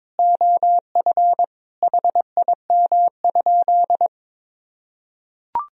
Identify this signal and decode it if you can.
Morse code, heard as a keyed tone, about 22 wpm, 700 Hz. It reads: OF HIM?
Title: Sentences from Top 1000 Words - Review Part 3 - 22wpm